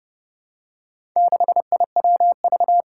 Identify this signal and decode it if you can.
Morse code, heard as a keyed tone, about 30 wpm, 700 Hz. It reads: 6IWV
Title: Four Challenging Letter-Number Combinations 30wpm